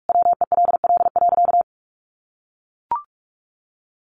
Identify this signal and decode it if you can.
Morse code, heard as a keyed tone, about 44 wpm, 700 Hz. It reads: WELL.
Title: Sentences from Top 800 Words 45wpm